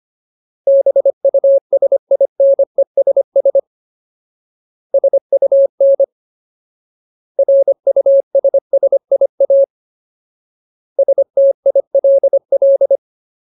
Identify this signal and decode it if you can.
Morse code, heard as a keyed tone, about 25 wpm, 550 Hz. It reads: BUSINESS SUN RUSSIA STILL
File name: Ultimate Word List / 25WPM Ultimate